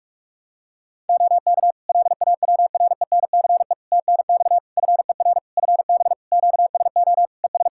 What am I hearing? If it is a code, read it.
Morse code, heard as a keyed tone, about 43 wpm, 700 Hz. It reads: OK LAWRENCE TNX FER FB QSO ES
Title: Practice QSO - Exchange 4 - Ending - 45wpm